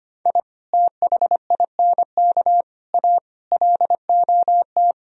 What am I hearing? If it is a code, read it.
Morse code, heard as a keyed tone, about 24 wpm, 700 Hz. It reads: I THINK A LOT
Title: Speed-Racing - Sets of 5 Words 25wpm